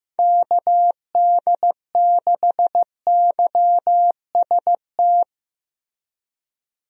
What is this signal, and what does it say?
Morse code, heard as a keyed tone, about 15 wpm, 700 Hz. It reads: KD6YST
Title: USA Call Signs - 6 Characters - Repeated 3x - 15wpm